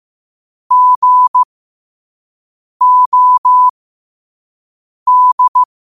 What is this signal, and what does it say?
Morse code, heard as a keyed tone, about 15 wpm, spaced as for 5 wpm, 1000 Hz. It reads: GOD
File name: qotdfiles / 5wpm